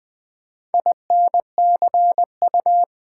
Morse code, heard as a keyed tone, about 20 wpm, 700 Hz. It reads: INCU
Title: Build A Word - 1 Letter At A Time - 3 to 5-Letter Words - 20wpm